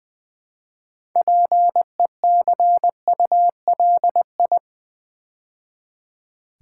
Morse code, heard as a keyed tone, about 20 wpm, 700 Hz. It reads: PECULI
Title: Build A Word - 2 Letters At A Time - 8-Letter Words - 20wpm